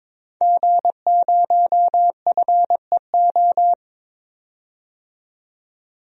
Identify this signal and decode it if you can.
Morse code, heard as a keyed tone, about 22 wpm, 700 Hz. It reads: G0FEO
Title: DXCC-100 Call Signs - 5 Characters - 22wpm